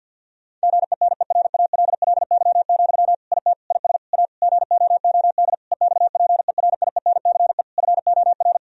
Morse code, heard as a keyed tone, about 50 wpm, 700 Hz. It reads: GENERALLY, IT IS A GOOD EXPERIENCE FOR
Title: Sentences from Top 1000 Words - Review Part 4 - 50wpm